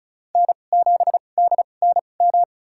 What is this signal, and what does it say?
Morse code, heard as a keyed tone, about 36 wpm, spaced as for 25 wpm, 700 Hz. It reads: N7DNM
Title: USA Call Signs - 5 Characters - 2x Character Spacing - 35wpm